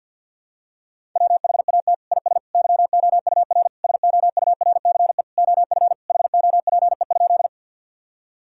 Morse code, heard as a keyed tone, about 50 wpm, 700 Hz. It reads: WHAT IS YOUR SOURCE OF HOPE?
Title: Sentences from Top 1000 Words - Review Part 5 - 50wpm